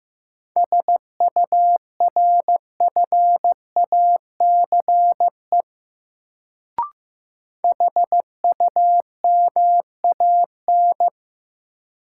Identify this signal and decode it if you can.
Morse code, heard as a keyed tone, about 15 wpm, 700 Hz. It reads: SURFACE HUMAN
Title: Sentences from Top 600 Words - New Word Drill - 15wpm